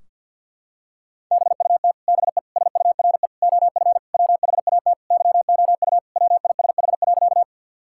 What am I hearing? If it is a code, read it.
Morse code, heard as a keyed tone, about 50 wpm, 700 Hz. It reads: BUT BE SURE OF WHAT YOU WISH.
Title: Sentences from Top 1600 Words 50wpm